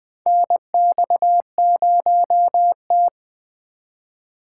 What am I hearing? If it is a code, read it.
Morse code, heard as a keyed tone, about 21 wpm, 700 Hz. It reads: NX0T
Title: USA Call Signs - 8x Character Spacing - 20wpm